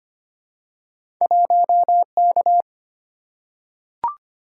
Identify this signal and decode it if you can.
Morse code, heard as a keyed tone, about 25 wpm, 700 Hz. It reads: 1K
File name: Two Letter-Number Combinations - 25wpm